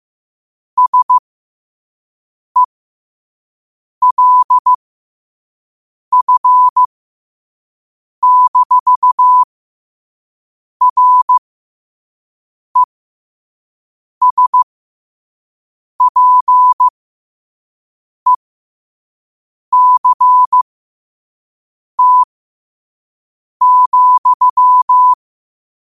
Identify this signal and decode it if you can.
Morse code, heard as a keyed tone, about 15 wpm, spaced as for 5 wpm, 1000 Hz. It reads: SELF-RESPECT,
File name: qotdfiles / 5wpm